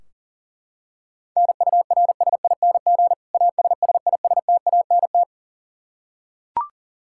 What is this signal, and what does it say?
Morse code, heard as a keyed tone, about 40 wpm, 700 Hz. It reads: NURSING ASSISTANT
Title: Practice QSO - Exchange 3 - More Info - 40wpm